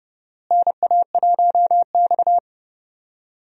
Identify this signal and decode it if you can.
Morse code, heard as a keyed tone, about 30 wpm, 700 Hz. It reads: NA1X
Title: USA Call Signs - 4 Characters - Repeated 2x - 30wpm